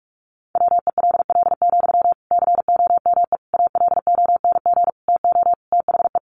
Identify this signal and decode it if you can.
Morse code, heard as a keyed tone, about 44 wpm, 700 Hz. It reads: WELL, COME ALONG TO THE
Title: Sentences from Top 1300 Words 45wpm